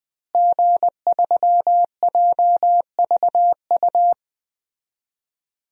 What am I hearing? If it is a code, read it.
Morse code, heard as a keyed tone, about 21 wpm, 700 Hz. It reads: G3JVU
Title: DXCC-100 Call Signs - 5 Characters - Repeated 2x - 20wpm